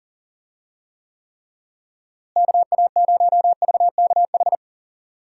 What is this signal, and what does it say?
Morse code, heard as a keyed tone, about 40 wpm, 700 Hz. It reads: KA0VKH
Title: USA Call Signs - 6 Characters - Repeated 3x - 40wpm